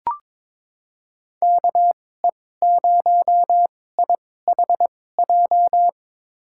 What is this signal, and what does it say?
Morse code, heard as a keyed tone, about 22 wpm, spaced as for 16 wpm, 700 Hz. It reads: KE0IHJ
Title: USA Call Signs - 6 Characters - 2x Character Spacing - 22wpm